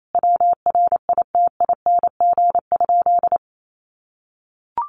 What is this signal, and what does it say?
Morse code, heard as a keyed tone, about 28 wpm, 700 Hz. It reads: WRITING?
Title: Sentences from Top 800 Words 28wpm